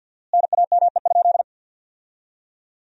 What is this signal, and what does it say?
Morse code, heard as a keyed tone, about 50 wpm, 700 Hz. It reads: NAME?
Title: 4x Word-Spacing - Sentences from Top 200 Words - 50wpm